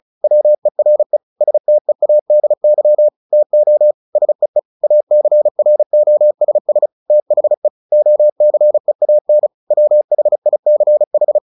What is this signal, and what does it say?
Morse code, heard as a keyed tone, about 34 wpm, 600 Hz. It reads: WERE STEADY TO SEE ACROSS THE OCEAN WHICH
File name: the old man and the sea 2 35wpm